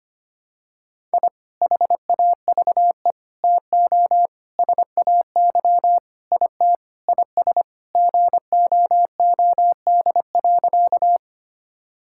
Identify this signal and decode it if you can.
Morse code, heard as a keyed tone, about 25 wpm, 700 Hz. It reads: I HAVE TO SAY IT IS GOOD.
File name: practice / 25wpm / Sentences with 4-Word Phrases - 25wpm